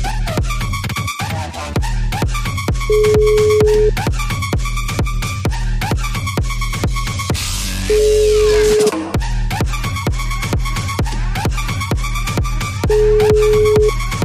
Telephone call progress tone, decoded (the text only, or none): ringback tone